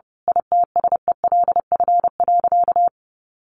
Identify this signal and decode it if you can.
Morse code, heard as a keyed tone, about 30 wpm, 700 Hz. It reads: ITSELF.
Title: A Sound Of Thunder 30wpm